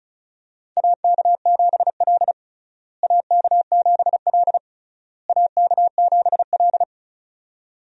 Morse code, heard as a keyed tone, about 36 wpm, 700 Hz. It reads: AK7L AK7L AK7L
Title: USA Call Signs - 4 Characters - Repeated 3x - 35wpm